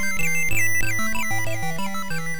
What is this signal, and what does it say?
There is no code.